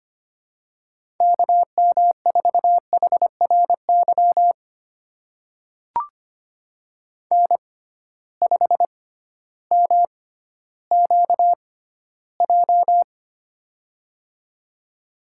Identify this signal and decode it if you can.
Morse code, heard as a keyed tone, about 25 wpm, spaced as for 14 wpm, 700 Hz. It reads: KM4HRY N5MQJ
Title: USA Call Signs - 6x Character Spacing - 25wpm